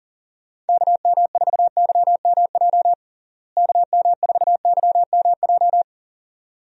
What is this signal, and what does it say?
Morse code, heard as a keyed tone, about 40 wpm, 700 Hz. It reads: KM4YMJ KM4YMJ
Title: USA Call Signs - 6 Characters - Repeated 2x - 40wpm